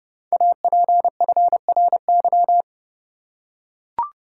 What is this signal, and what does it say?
Morse code, heard as a keyed tone, about 30 wpm, 700 Hz. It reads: APFRY
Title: Code Groups - 5 Letters - 30wpm